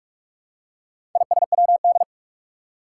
Morse code, heard as a keyed tone, about 46 wpm, 700 Hz. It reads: ISWD
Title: Four Challenging Letter-Number Combinations 45wpm